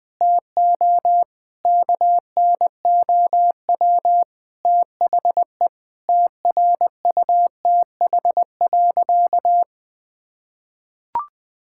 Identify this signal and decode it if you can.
Morse code, heard as a keyed tone, about 20 wpm, 700 Hz. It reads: TO KNOW THE TRUTH.
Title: Sentences with 7-Word Phrases - Spoken Beginning - 20wpm